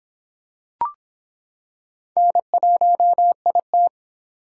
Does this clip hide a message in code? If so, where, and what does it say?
Yes: Morse code from 2.2 to 3.9 s, heard as a keyed tone, about 26 wpm, 700 Hz. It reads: N1IT